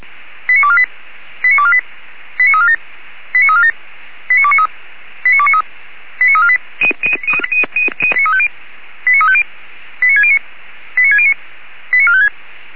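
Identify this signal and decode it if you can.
Five-tone selective call (ZVEI), heard as quick runs of five tones, tones of about 70 ms. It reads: 89248 89248 89347 89347 89293 89293 89348 89350 89609 89609 89456
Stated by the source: ZVEI five-tone selective call received on an SDR.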